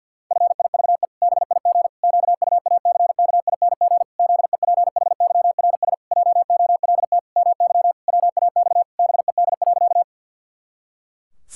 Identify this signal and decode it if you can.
Morse code, heard as a keyed tone, about 52 wpm, 700 Hz. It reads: FIVE BIG QUACKING ZEPHYRS JOLT MY WAX BED.
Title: Pangram Challenge v2 - 50wpm